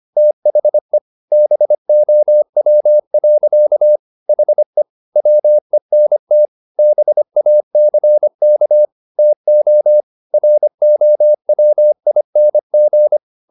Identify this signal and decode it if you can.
Morse code, heard as a keyed tone, about 25 wpm, 600 Hz. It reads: THE BOW. HE WENT BACK TO ROWING
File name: the old man and the sea part 1 25wpm